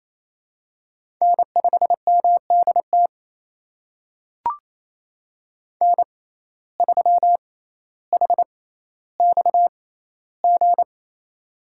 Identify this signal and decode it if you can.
Morse code, heard as a keyed tone, about 28 wpm, spaced as for 14 wpm, 700 Hz. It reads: N5MDT N3HXG
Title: USA Call Signs - 5 Characters - 6x Character Spacing - 28wpm